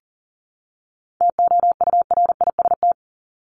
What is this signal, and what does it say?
Morse code, heard as a keyed tone, about 40 wpm, 700 Hz. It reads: TOURIST
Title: Top 1501-1750 Words 40wpm